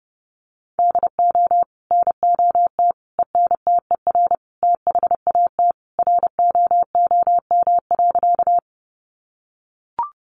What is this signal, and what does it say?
Morse code, heard as a keyed tone, about 30 wpm, 700 Hz. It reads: DO NOT ENTER THAT ROOM.
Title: Sentences from Top 1300 Words 30wpm